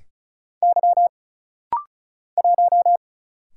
Morse code, heard as a keyed tone, about 33 wpm, 700 Hz. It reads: Y 1
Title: Single Letter-Number Alt - Rapid-Fire 35wpm